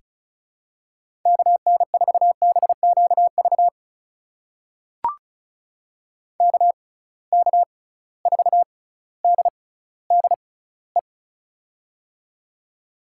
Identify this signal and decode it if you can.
Morse code, heard as a keyed tone, about 36 wpm, spaced as for 18 wpm, 700 Hz. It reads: KN4BQV KK4DDE
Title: USA Call Signs - 6x Character Spacing - 35wpm